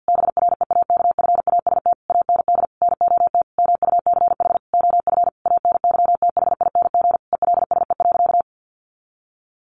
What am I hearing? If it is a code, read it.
Morse code, heard as a keyed tone, about 50 wpm, 700 Hz. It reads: BREAKFAST AND NOT MUCH OF ANYTHING ELSE.
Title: Sentences from Top 1500 Words 50wpm